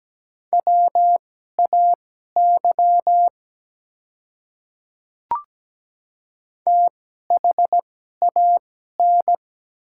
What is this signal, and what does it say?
Morse code, heard as a keyed tone, about 17 wpm, spaced as for 12 wpm, 700 Hz. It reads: WAY THAN
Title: 2x Character Spacing - Top 100 Words - 17wpm